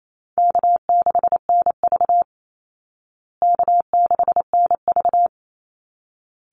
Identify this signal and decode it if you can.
Morse code, heard as a keyed tone, about 28 wpm, 700 Hz. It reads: K6NV K6NV
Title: Easy Call Signs Repeated 2x - 28wpm